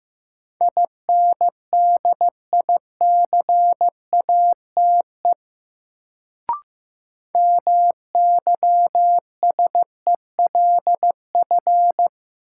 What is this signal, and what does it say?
Morse code, heard as a keyed tone, about 15 wpm, 700 Hz. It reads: INDICATE MYSELF
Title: Sentences from Top 1300 Words - New Word Drill - 15wpm